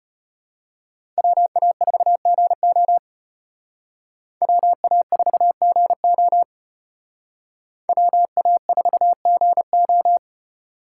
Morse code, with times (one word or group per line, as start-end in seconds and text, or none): 1.2-3.0 WA4GO
4.4-6.4 WA4GO
7.9-10.2 WA4GO